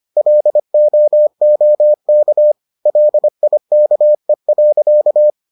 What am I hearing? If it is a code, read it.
Morse code, heard as a keyed tone, about 25 wpm, 600 Hz. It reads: LOOK LIKE.